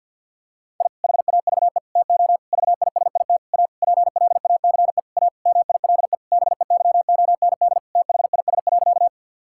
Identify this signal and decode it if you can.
Morse code, heard as a keyed tone, about 52 wpm, 700 Hz. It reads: I HAVE TO VISIT A PLACE A MILE BEYOND THIS.
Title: Sentences from Top 1000 Words - Review Part 6 - 50wpm